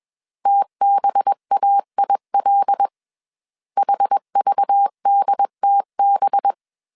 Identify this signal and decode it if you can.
Morse code, heard as a keyed tone, about 21 wpm, 800 Hz. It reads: TBAIL HVDTB
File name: CW / DET / 1 6 1500C S B 20wpm 2048